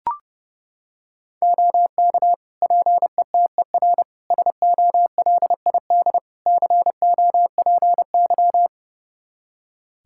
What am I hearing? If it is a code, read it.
Morse code, heard as a keyed tone, about 30 wpm, 700 Hz. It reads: OK PETER SOLID COPY
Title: K1USN SST QSO Practice 30wpm